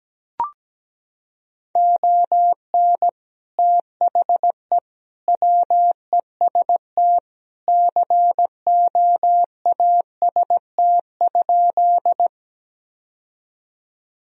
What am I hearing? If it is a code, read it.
Morse code, heard as a keyed tone, about 17 wpm, 700 Hz. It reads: ON THE WEST COAST?
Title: Sentences from Top 1000 Words - Review Part 4 - 17wpm